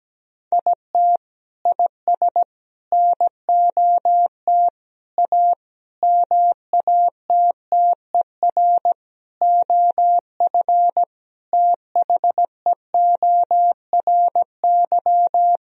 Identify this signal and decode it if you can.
Morse code, heard as a keyed tone, about 17 wpm, 700 Hz. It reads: IT IS NOT A MATTER OF THEORY
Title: Sentences from Top 1000 Words - Review Part 4 - 17wpm